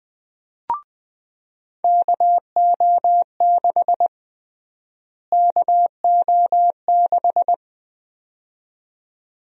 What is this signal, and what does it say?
Morse code, heard as a keyed tone, about 20 wpm, 700 Hz. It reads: KO6 KO6
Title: USA Call Sign Prefixes - Repeated 2x - 20wpm